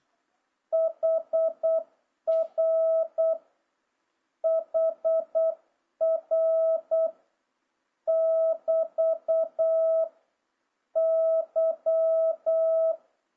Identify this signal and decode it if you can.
Morse code, heard as a keyed tone, about 8 wpm, 650 Hz. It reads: HR HR <BT> Y
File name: jh8cbh / 2026cw